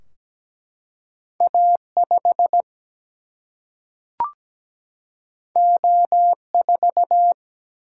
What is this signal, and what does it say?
Morse code, heard as a keyed tone, about 17 wpm, 700 Hz. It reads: A5 O4
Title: Single-Letter Single-Number 17wpm